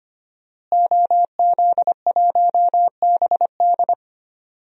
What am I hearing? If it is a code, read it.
Morse code, heard as a keyed tone, about 25 wpm, 700 Hz. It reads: OZ1BD
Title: DXCC-100 Call Signs - 5 Characters - Repeated 2x - 25wpm